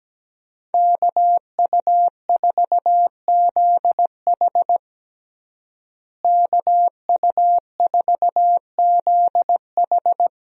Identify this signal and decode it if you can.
Morse code, heard as a keyed tone, about 17 wpm, 700 Hz. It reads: KU4ZH KU4ZH
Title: USA Call Signs - 5 Characters - Repeated 3x - 17wpm